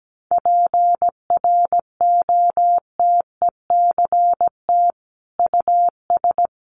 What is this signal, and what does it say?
Morse code, heard as a keyed tone, about 17 wpm, 700 Hz. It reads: PROTECT US
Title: Sentences from Top 900 Words 17wpm